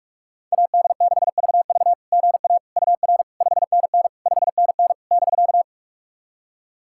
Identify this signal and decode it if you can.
Morse code, heard as a keyed tone, about 46 wpm, 690 Hz. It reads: AD6VV GA UR 5NN 5NN <BK>
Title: SOTA QSO Practice 45wpm